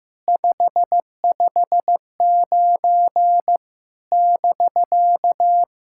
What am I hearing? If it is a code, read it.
Morse code, heard as a keyed tone, about 15 wpm, 700 Hz. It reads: 559 <BK>
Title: SOTA QSO Practice 15wpm